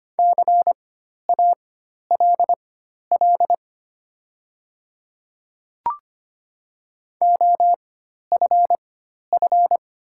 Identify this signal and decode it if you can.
Morse code, heard as a keyed tone, about 25 wpm, spaced as for 12 wpm, 690 Hz. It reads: CALL OFF
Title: 4x Character Spacing - Top 200 Words - 25wpm